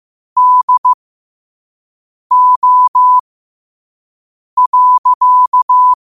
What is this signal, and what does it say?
Morse code, heard as a keyed tone, about 15 wpm, spaced as for 5 wpm, 1000 Hz. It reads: DO.
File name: qotdfiles / 5wpm